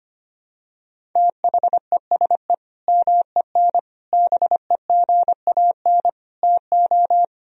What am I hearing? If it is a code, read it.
Morse code, heard as a keyed tone, about 25 wpm, 700 Hz. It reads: THESE MEN BEGAN TO
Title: Sentences from Top 1000 Words - Review Part 1 - 25wpm